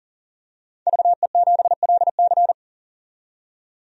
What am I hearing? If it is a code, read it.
Morse code, heard as a keyed tone, about 40 wpm, 700 Hz. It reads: VE7LC
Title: DXCC-100 Call Signs - 5 Characters - 40wpm